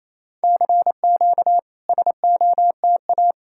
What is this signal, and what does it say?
Morse code, heard as a keyed tone, about 28 wpm, 700 Hz. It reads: CQ SOTA
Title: SOTA QSO Practice 28wpm